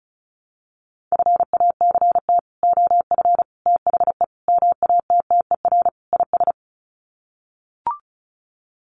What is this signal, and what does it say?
Morse code, heard as a keyed tone, about 34 wpm, 700 Hz. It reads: FACT OF THE MATTER IS
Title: Sets of 5 Words 35wpm